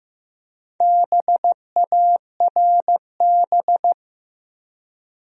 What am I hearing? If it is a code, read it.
Morse code, heard as a keyed tone, about 15 wpm, 700 Hz. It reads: BARB
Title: USA Common Names - Max 5 Letters - 15wpm